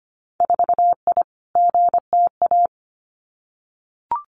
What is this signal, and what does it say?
Morse code, heard as a keyed tone, about 25 wpm, 700 Hz. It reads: 4I GTA
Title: Winter Field Day Exchanges Encore 25wpm